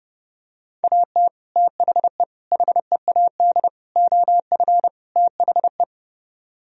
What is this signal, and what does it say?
Morse code, heard as a keyed tone, about 31 wpm, 700 Hz. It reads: AT THE HEAD OF THE